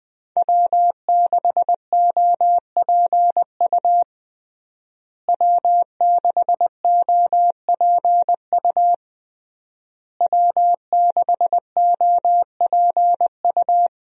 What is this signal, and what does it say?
Morse code, heard as a keyed tone, about 21 wpm, 700 Hz. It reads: W6OPU W6OPU W6OPU
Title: USA Call Signs - 5 Characters - Repeated 3x - 20wpm